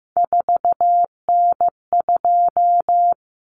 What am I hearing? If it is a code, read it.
Morse code, heard as a keyed tone, about 15 wpm, 700 Hz. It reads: 4N2
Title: Common Cut Numbers - 3-Digit 15wpm